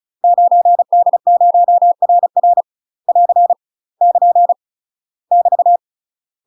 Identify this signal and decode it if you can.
Morse code, heard as a keyed tone, about 36 wpm, 700 Hz. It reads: 9D0RR <AR> <KN> <BT>
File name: UK MASTER QSOs TRAINING / 35WPM / UK MASTER QSOs 0017